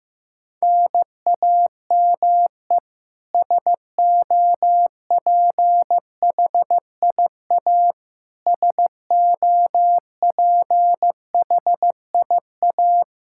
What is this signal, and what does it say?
Morse code, heard as a keyed tone, about 15 wpm, 700 Hz. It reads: NAME SOPHIA SOPHIA
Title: Practice QSO - Exchange 1 - Beginning - 15wpm